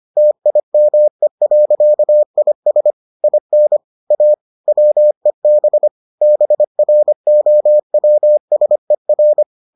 Morse code, heard as a keyed tone, about 25 wpm, 600 Hz. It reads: TIME.IS IN A WEB BROWSER